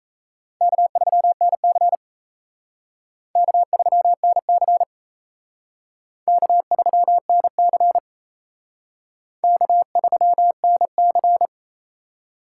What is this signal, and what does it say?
Morse code, heard as a keyed tone, about 35 wpm, 700 Hz. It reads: K3NC K3NC K3NC K3NC
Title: Speed Racing - USA 1x2 Call Signs 28wpm